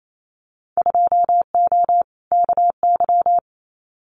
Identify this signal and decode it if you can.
Morse code, heard as a keyed tone, about 28 wpm, 700 Hz. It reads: 2O KY
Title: Winter Field Day Exchanges 28wpm